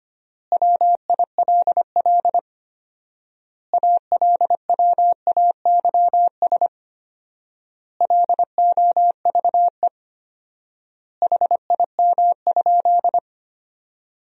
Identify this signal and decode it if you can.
Morse code, heard as a keyed tone, about 25 wpm, 700 Hz. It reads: WILL ALWAYS LOVE HIM?
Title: 4x Word-Spacing - Sentences from Top 400 Words - 25wpm